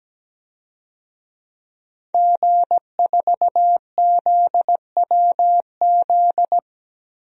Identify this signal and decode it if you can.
Morse code, heard as a keyed tone, about 17 wpm, 700 Hz. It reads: G4ZWZ
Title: DXCC-100 Call Signs - 5 Characters - 17wpm